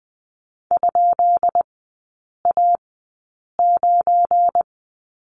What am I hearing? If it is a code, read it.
Morse code, heard as a keyed tone, about 20 wpm, spaced as for 8 wpm, 700 Hz. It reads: ?A9
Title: 172 Lesson 28e - Number 9 - Three Characters 20wpm